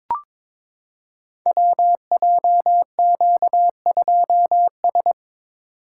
Morse code, heard as a keyed tone, about 22 wpm, 700 Hz. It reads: WJQ2S
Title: Code Groups - 5 Letter:Numbers - 22wpm